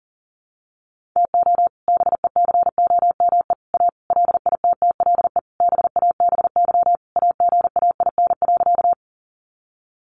Morse code, heard as a keyed tone, about 40 wpm, 700 Hz. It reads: TO BECOME A LITTLE BABY AGAIN.
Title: Sentences from Top 1300 Words 40wpm